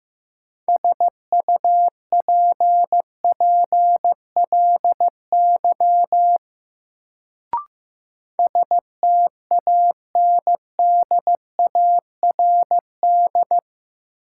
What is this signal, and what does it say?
Morse code, heard as a keyed tone, about 15 wpm, 700 Hz. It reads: SUPPLY STANDARD
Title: Sentences from Top 1200 Words - New Word Drill Alt - 15wpm